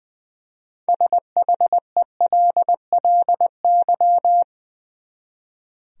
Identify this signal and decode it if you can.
Morse code, heard as a keyed tone, about 20 wpm, 700 Hz. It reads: SHELLY
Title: USA Common Names - 20wpm